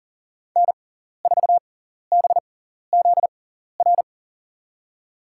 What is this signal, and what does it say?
Morse code, heard as a keyed tone, about 40 wpm, spaced as for 14 wpm, 700 Hz. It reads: N4BZR
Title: USA Call Signs - 5 Characters - 6x Character Spacing - 40wpm